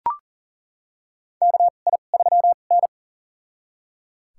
Morse code, heard as a keyed tone, about 38 wpm, spaced as for 28 wpm, 700 Hz. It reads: KI3N
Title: USA Call Signs - 4 Characters - 2x Character Spacing - 40wpm